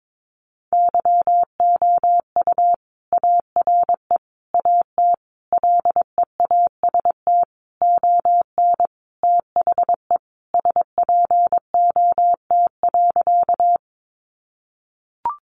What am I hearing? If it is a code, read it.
Morse code, heard as a keyed tone, about 22 wpm, 700 Hz. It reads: YOU ARE AT LEAST ON THE SPOT.
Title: Sentences from Top 1100 Words 22wpm